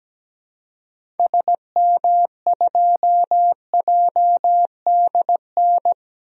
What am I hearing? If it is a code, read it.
Morse code, heard as a keyed tone, about 17 wpm, 700 Hz. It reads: SM2JDN
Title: DXCC-100 Call Signs - 6 Characters - Repeated 2x - 17wpm